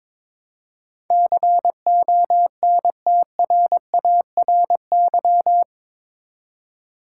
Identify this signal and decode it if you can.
Morse code, heard as a keyed tone, about 22 wpm, 700 Hz. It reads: CONTRARY
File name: practice / 22wpm / Eight Letter Words - 22wpm